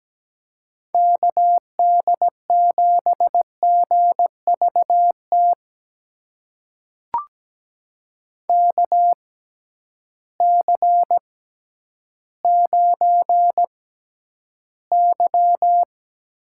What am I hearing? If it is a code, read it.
Morse code, heard as a keyed tone, about 17 wpm, spaced as for 10 wpm, 700 Hz. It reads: KD7GVT KC9Y
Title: USA Call Signs - 6x Character Spacing - 17wpm